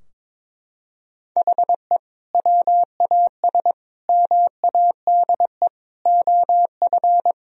Morse code, heard as a keyed tone, about 22 wpm, 700 Hz. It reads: HE WAS MADE OF
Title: Sentences from Top 1000 Words - Review Part 4 - 22wpm